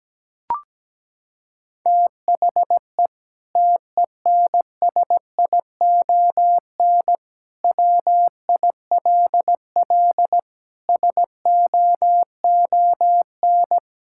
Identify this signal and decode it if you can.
Morse code, heard as a keyed tone, about 17 wpm, 700 Hz. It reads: THE TENSION WILL SOON
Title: Sentences from Top 1600 Words 17wpm